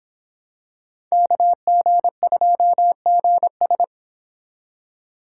Morse code, heard as a keyed tone, about 25 wpm, 700 Hz. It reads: KG2GS